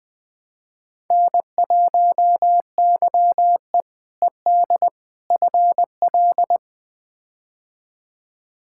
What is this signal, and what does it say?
Morse code, heard as a keyed tone, about 21 wpm, 700 Hz. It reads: N1YE ED FL
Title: K1USN SST QSO Practice - 20wpm